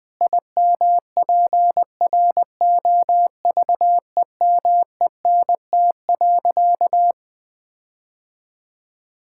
Morse code, heard as a keyed tone, about 20 wpm, 700 Hz. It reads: IMPROVEMENT.
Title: Sentences with 8-Word Phrases - 20wpm